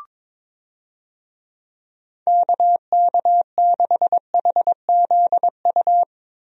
Morse code, heard as a keyed tone, about 21 wpm, 700 Hz. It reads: KK6HZU
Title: USA Call Signs - 6 Characters - Repeated 3x - 22wpm